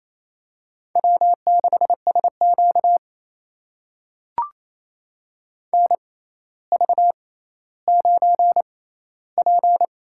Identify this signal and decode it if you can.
Morse code, heard as a keyed tone, about 28 wpm, spaced as for 14 wpm, 700 Hz. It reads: W6SQ NV9P
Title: USA Call Signs - 4 Characters - 6x Character Spacing - 28wpm